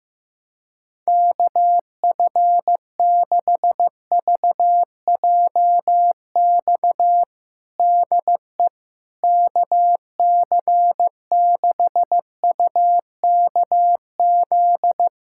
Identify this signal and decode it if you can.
Morse code, heard as a keyed tone, about 15 wpm, 700 Hz. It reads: KF6VJX DE KC6UKZ
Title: USA Callsign DE Callsign - 6 Characters - 15wpm